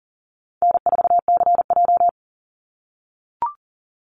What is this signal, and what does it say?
Morse code, heard as a keyed tone, about 40 wpm, 700 Hz. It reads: N4CJ
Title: Call Signs - 40wpm